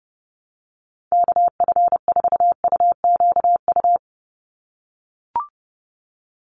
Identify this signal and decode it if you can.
Morse code, heard as a keyed tone, about 30 wpm, 700 Hz. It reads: KF4UQU
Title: USA Call Signs 30wpm